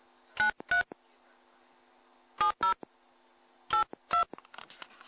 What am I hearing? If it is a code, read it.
DTMF, heard as telephone touch tones, tones of about 110 ms. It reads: 93 70 82